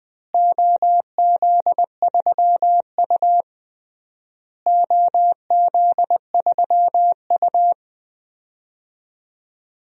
Morse code, heard as a keyed tone, about 20 wpm, 700 Hz. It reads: OZ3U OZ3U
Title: DXCC-100 Call Signs - 4 Characters - Repeated 3x - 20wpm